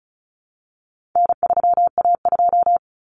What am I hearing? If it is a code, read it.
Morse code, heard as a keyed tone, about 34 wpm, 700 Hz. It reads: N3A2
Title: Common Cut Numbers - 4-Digit 35wpm